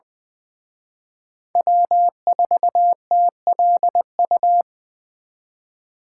Morse code, heard as a keyed tone, about 24 wpm, 700 Hz. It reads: W4TLU
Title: Speed Racing - USA 1x3 Call Signs 20wpm